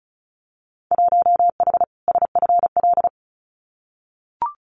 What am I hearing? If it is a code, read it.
Morse code, heard as a keyed tone, about 34 wpm, 700 Hz. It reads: 1H SFL
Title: Winter Field Day Exchanges 35wpm